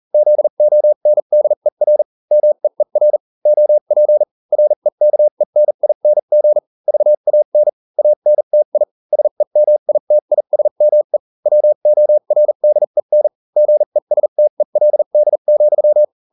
Morse code, heard as a keyed tone, about 40 wpm, spaced as for 35 wpm, 600 Hz. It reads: ZONDER MEER OP REKENING VAN ANTI SEMITISME WORDEN GESTELD,